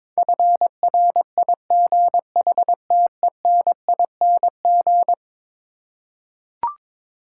Morse code, heard as a keyed tone, about 22 wpm, 700 Hz. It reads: FRIGHTENING
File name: practice / 22wpm / Eleven Letter Words - 22wpm